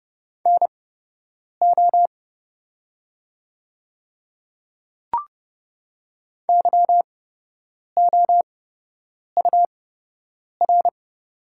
Morse code, heard as a keyed tone, about 30 wpm, spaced as for 8 wpm, 700 Hz. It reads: NO YOUR
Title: 8x Character Spacing - Top 100 Words - 30wpm